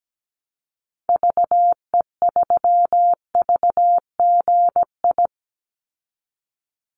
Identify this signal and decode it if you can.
Morse code, heard as a keyed tone, about 17 wpm, 700 Hz. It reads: VE3VGI
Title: Call Signs Repeated 3x - 17wpm